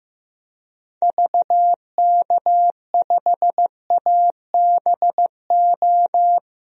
Morse code, heard as a keyed tone, about 15 wpm, 700 Hz. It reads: VK5ABO
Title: DXCC-100 Call Signs - 6 Characters - 15wpm